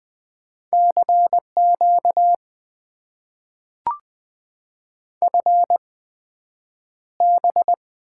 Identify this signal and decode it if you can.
Morse code, heard as a keyed tone, about 20 wpm, spaced as for 9 wpm, 700 Hz. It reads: CQ FB
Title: Top 90 QSO Elements - 8x Character Spacing - 20wpm